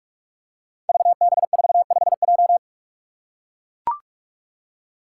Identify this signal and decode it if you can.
Morse code, heard as a keyed tone, about 44 wpm, 700 Hz. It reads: VB45J
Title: Code Groups - 5 Letter:Numbers - 45wpm